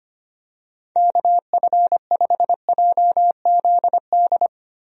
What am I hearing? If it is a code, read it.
Morse code, heard as a keyed tone, about 25 wpm, 700 Hz. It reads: KF5JZD
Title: USA Call Signs - 6 Characters - Repeated 2x - 25wpm